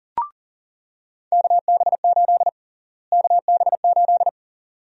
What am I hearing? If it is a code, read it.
Morse code, heard as a keyed tone, about 40 wpm, 700 Hz. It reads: KB8 KB8
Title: USA Call Sign Prefixes - Repeated 2x - 40wpm